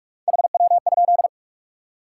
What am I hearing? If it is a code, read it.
Morse code, heard as a keyed tone, about 55 wpm, 700 Hz. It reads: HW?
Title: Speed-Racing Top 90 QSO Elements 45wpm